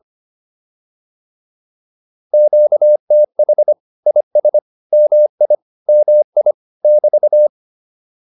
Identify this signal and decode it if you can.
Morse code, heard as a keyed tone, about 25 wpm, 600 Hz. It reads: QTH IS MI MI <BT>